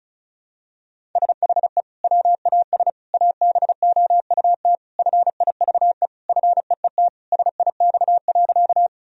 Morse code, heard as a keyed tone, about 36 wpm, 700 Hz. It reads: SHE WAS ABOUT FIVE FEET SIX.
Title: Sentences from Top 1000 Words - Review Part 2 - 35wpm